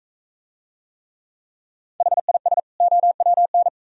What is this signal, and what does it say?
Morse code, heard as a keyed tone, about 42 wpm, 700 Hz. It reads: HIS OWN